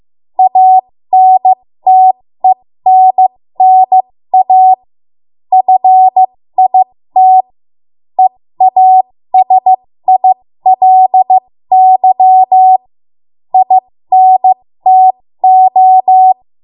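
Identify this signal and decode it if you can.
Morse code, heard as a keyed tone, about 15 wpm, spaced as for 13 wpm, 750 Hz. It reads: ANTENNA FIT EASILY INTO
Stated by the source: ARRL Morse code practice at 13 WPM.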